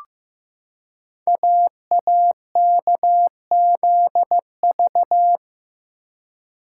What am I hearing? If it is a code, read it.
Morse code, heard as a keyed tone, about 15 wpm, 700 Hz. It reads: AAKZV
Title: Code Groups - 5 Letter:Numbers - 15wpm